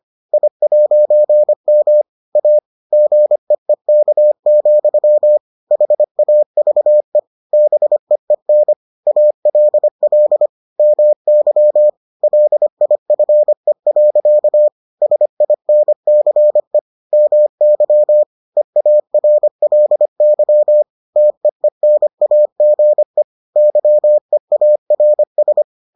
Morse code, heard as a keyed tone, about 25 wpm, 600 Hz. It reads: I'M A GEEK, HAVE BEEN ALL MY LIFE. SINCE MY EARLY TEENAGE YEARS